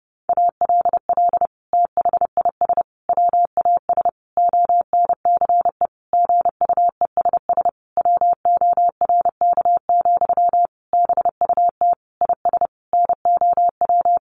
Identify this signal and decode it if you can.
Morse code, heard as a keyed tone, about 30 wpm, 700 Hz. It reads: ALL THIS WAS ONCE GUESS WORK, BUT IS NOW
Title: Sentences from Top 1500 Words 30wpm